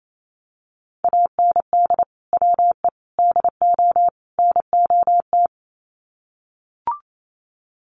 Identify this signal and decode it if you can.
Morse code, heard as a keyed tone, about 28 wpm, 700 Hz. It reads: AND WE DO NOT
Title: Sets of 4 Words 28wpm